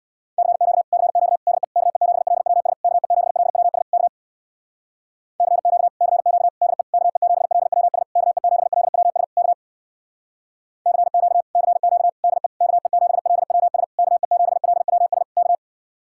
Morse code, heard as a keyed tone, about 61 wpm, 700 Hz. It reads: CQ CQ DE KE7LWS KE7LWS K CQ CQ DE KE7LWS KE7LWS K CQ CQ DE KE7LWS KE7LWS K
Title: Speed-Racing Common QSO Phrases - 50wpm